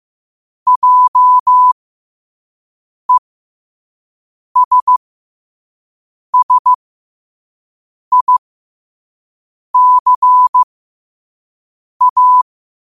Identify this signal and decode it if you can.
Morse code, heard as a keyed tone, about 15 wpm, spaced as for 5 wpm, 1000 Hz. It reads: JESSICA
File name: qotdfiles / 5wpm